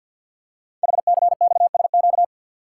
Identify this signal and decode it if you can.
Morse code, heard as a keyed tone, about 50 wpm, 700 Hz. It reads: HCXSQ
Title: Code Groups - 5 Letters - 50wpm